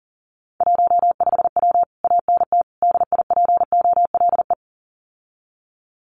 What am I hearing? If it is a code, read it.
Morse code, heard as a keyed tone, about 40 wpm, 700 Hz. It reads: 15W ANT DIPOLE
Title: Common QSO Phrases Encore 40wpm